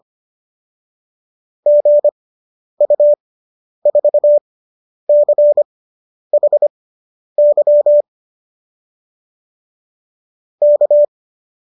Morse code, heard as a keyed tone, about 25 wpm, spaced as for 10 wpm, 570 Hz. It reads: GU4CHY K